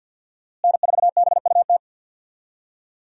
Morse code, heard as a keyed tone, about 52 wpm, 700 Hz. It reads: N4BUT
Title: USA Call Signs - 5 Characters - Repeated 3x - 50wpm